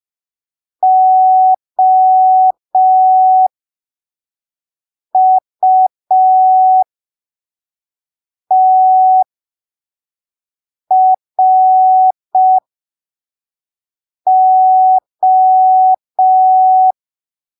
Morse code, heard as a keyed tone, about 5 wpm, spaced as for 3 wpm, 750 Hz. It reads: OUTRO
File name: arquivos / texto / texto 15 05ppm